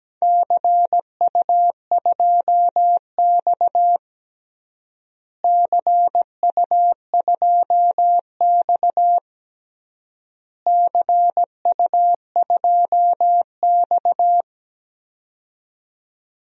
Morse code, heard as a keyed tone, about 17 wpm, 700 Hz. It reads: CU2X CU2X CU2X
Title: DXCC-100 Call Signs - 4 Characters - Repeated 3x - 17wpm